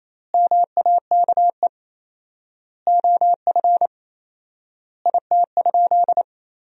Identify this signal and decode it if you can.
Morse code, heard as a keyed tone, about 28 wpm, 700 Hz. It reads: MAKE OF IT?
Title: 4x Word-Spacing - Sentences from Top 100 Words - 28wpm